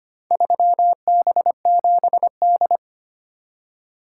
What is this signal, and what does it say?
Morse code, heard as a keyed tone, about 25 wpm, 700 Hz. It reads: 3B7D
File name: practice / 25wpm / Four Letter-Number Combinations - 25wpm